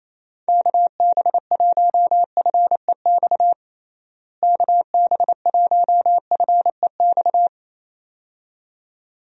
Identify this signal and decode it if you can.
Morse code, heard as a keyed tone, about 27 wpm, 700 Hz. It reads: KB1FEX KB1FEX
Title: USA Call Signs - 6 Characters - Repeated 2x - 28wpm